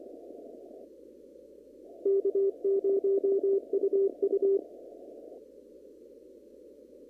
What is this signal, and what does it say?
Morse code, heard as a keyed tone, about 24 wpm, 390 Hz. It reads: K0UU
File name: CW / kt5j wpx cw-op-k5pi